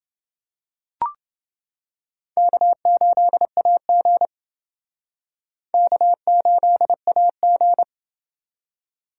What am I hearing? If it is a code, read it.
Morse code, heard as a keyed tone, about 28 wpm, 700 Hz. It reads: K8AG K8AG